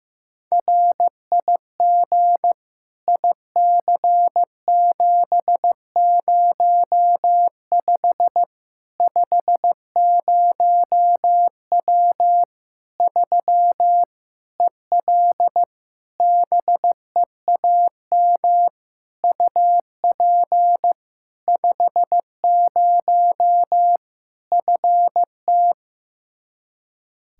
Morse code, heard as a keyed tone, about 15 wpm, 700 Hz. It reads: RIG IC705 50W 3 EL BEAM UP 50 FT
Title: Practice QSO - Exchange 2 - Info - 15wpm